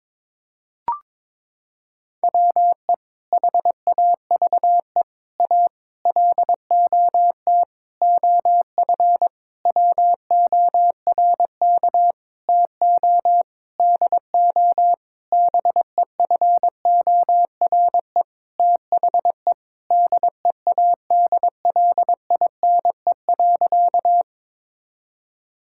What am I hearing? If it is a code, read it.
Morse code, heard as a keyed tone, about 22 wpm, 700 Hz. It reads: WE HAVE A LOT OF WORK TO DO BEFORE THE DEADLINE.
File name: practice / 22wpm / Sentences with 8-Word Phrases - 22wpm